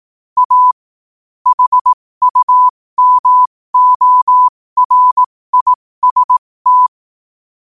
Morse code, heard as a keyed tone, about 18 wpm, spaced as for 15 wpm, 1000 Hz. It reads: A HUMORIST
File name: qotdfiles / 15wpm